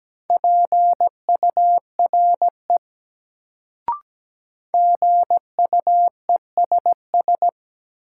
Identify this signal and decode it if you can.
Morse code, heard as a keyed tone, about 17 wpm, 700 Hz. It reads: PURE GUESS
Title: Sentences from Top 1500 Words - New Word Drill Alt - 17wpm